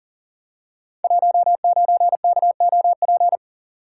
Morse code, heard as a keyed tone, about 42 wpm, 700 Hz. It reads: 19KOP
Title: Code Groups - 5 Letter:Numbers - 40wpm